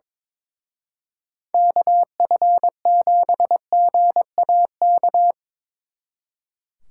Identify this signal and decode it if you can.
Morse code, heard as a keyed tone, about 24 wpm, 700 Hz. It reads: KF7GAK
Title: Speed Racing - USA 2x3 Call Signs 22wpm